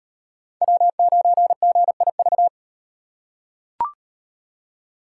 Morse code, heard as a keyed tone, about 34 wpm, 700 Hz. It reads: W9GIV